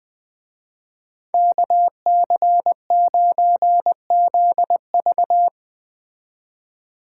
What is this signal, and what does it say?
Morse code, heard as a keyed tone, about 20 wpm, 700 Hz. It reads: KC9ZV
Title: Speed Racing - USA 2x2 Call Signs 17wpm